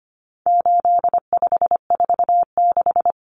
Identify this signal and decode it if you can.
Morse code, heard as a keyed tone, about 25 wpm, 700 Hz. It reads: 8546
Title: Common Cut Numbers - 4-Digit 25wpm